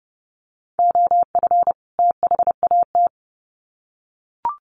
Morse code, heard as a keyed tone, about 30 wpm, 700 Hz. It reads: OF THAT
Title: Sets of 2 Words 30wpm